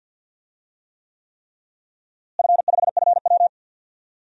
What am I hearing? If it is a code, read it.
Morse code, heard as a keyed tone, about 52 wpm, 700 Hz. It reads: F5FW
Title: DXCC-100 Call Signs - 4 Characters - Repeated 3x - 50wpm